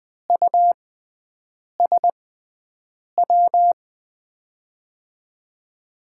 Morse code, heard as a keyed tone, about 20 wpm, spaced as for 7 wpm, 700 Hz. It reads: USW
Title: USA Call Signs - 6 Characters - 6x Character Spacing - 20wpm